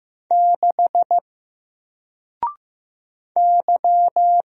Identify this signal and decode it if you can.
Morse code, heard as a keyed tone, about 15 wpm, 700 Hz. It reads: 6 Y
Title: Single Letter-Number 15wpm